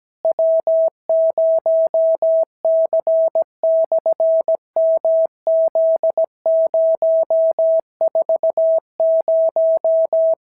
Morse code, heard as a keyed tone, about 17 wpm, 650 Hz. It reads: W0C/MZ040
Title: SOTA QSO Practice 17wpm